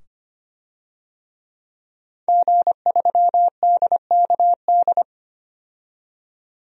Morse code, heard as a keyed tone, about 25 wpm, 700 Hz. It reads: G3DKD
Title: DXCC-100 Call Signs - 5 Characters - 25wpm